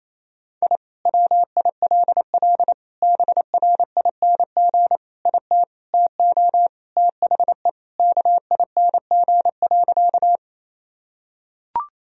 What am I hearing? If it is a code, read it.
Morse code, heard as a keyed tone, about 28 wpm, 700 Hz. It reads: I WILL BRING IT TO THE KING.
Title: Sentences from Top 1000 Words - Review Part 3 - 28wpm